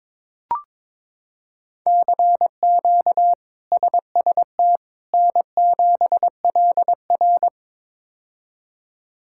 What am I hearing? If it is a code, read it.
Morse code, heard as a keyed tone, about 22 wpm, 700 Hz. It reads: CQ SST N7LR
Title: K1USN SST QSO Practice - 22wpm